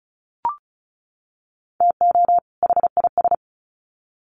Morse code, heard as a keyed tone, about 34 wpm, 700 Hz. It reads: TO HIS
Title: Sets of 2 Words 35wpm